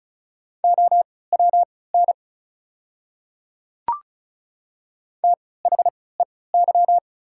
Morse code, heard as a keyed tone, about 34 wpm, spaced as for 20 wpm, 690 Hz. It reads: OWN THEY
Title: 3x Character Spacing - Top 300 Words - 35wpm